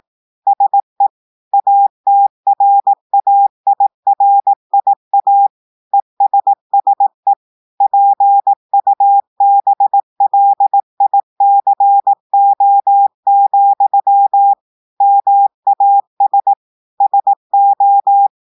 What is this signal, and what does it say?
Morse code, heard as a keyed tone, about 18 wpm, 800 Hz. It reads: SE ATRAIRIA ESSE PUBLICO, MAS SO
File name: portugais texte en clair 2 vitesse 18 mots minutes